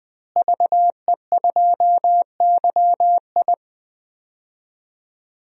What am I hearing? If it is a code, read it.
Morse code, heard as a keyed tone, about 20 wpm, 700 Hz. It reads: VE2YI
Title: DXCC-100 Call Signs - 5 Characters - Repeated 3x - 20wpm